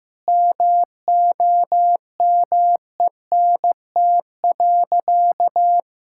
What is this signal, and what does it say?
Morse code, heard as a keyed tone, about 16 wpm, 700 Hz. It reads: MOMENT.
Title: Sentences from Top 1000 Words - Review Part 5 - 15wpm